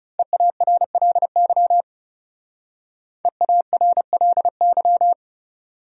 Morse code, heard as a keyed tone, about 36 wpm, 700 Hz. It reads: EARLY EARLY
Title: Speed-Racing Top 500 Words 30wpm